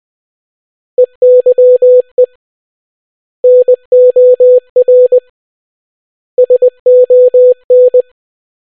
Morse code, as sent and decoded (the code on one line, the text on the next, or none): . -.-- . / -. --- .-. / ... --- -.
EYE NOR SON